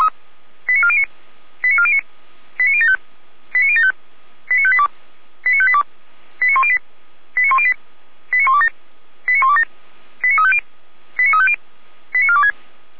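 Five-tone selective call (ZVEI), heard as quick runs of five tones, tones of about 70 ms. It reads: 89409 89409 89075 89075 89582 89582 89108 89108 89127 89127 89350 89350 89426
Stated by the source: Received on an SDR as ZVEI five-tone call, address 89018.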